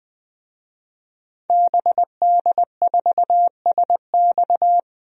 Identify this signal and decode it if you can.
Morse code, heard as a keyed tone, about 20 wpm, 700 Hz. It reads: BD4SX
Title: DXCC-100 Call Signs - 5 Characters - 20wpm